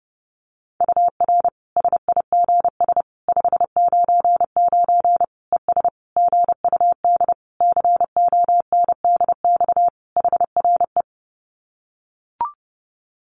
Morse code, heard as a keyed tone, about 30 wpm, 700 Hz. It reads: UR SIGS 599 ES GUD CONDX HRE
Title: Common QSO Phrases 30wpm